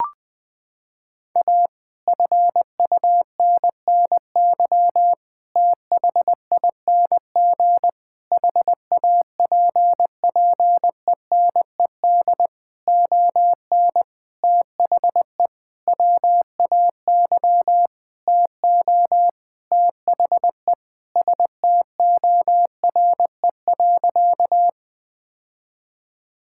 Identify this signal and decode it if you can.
Morse code, heard as a keyed tone, about 20 wpm, 700 Hz. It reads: A FUNNY THING HAPPENED ON THE WAY TO THE STORE.
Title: Sentences with 9-Word Phrases - 20wpm